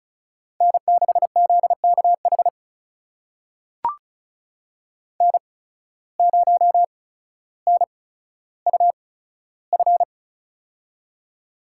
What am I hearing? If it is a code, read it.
Morse code, heard as a keyed tone, about 34 wpm, spaced as for 15 wpm, 700 Hz. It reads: N6ZKH N0NUF
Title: USA Call Signs - 5 Characters - 8x Character Spacing - 35wpm